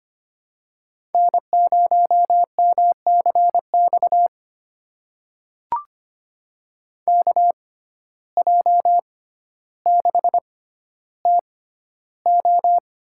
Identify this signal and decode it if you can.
Morse code, heard as a keyed tone, about 25 wpm, spaced as for 13 wpm, 700 Hz. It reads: N0MCX KJ6TO
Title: USA Call Signs - 5 Characters - 6x Character Spacing - 25wpm